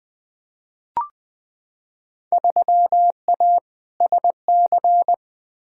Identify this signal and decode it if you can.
Morse code, heard as a keyed tone, about 20 wpm, 700 Hz. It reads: 3A SC
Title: Field Day Exchanges - 20wpm